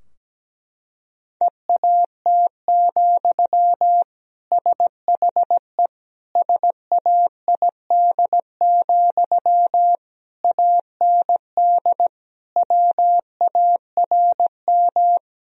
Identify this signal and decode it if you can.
Morse code, heard as a keyed tone, about 17 wpm, 700 Hz. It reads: EAT, SHE SAID, AND WARM
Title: Sentences from Top 1000 Words - Review Part 2 - 17wpm